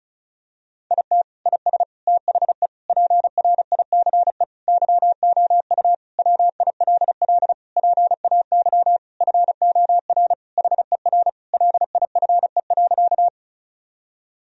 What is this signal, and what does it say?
Morse code, heard as a keyed tone, about 36 wpm, 700 Hz. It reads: IT IS THE PRICE YOU WILL PAY FOR HER LIFE.
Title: Sentences from Top 1000 Words - Review Part 4 - 35wpm